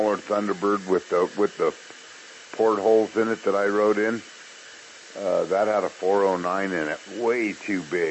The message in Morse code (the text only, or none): none